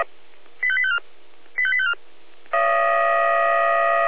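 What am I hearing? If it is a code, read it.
Five-tone selective call (ZVEI), heard as quick runs of five tones, tones of about 70 ms. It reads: 85854 85854
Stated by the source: ZVEI five-tone selective call 85850 received on an SDR.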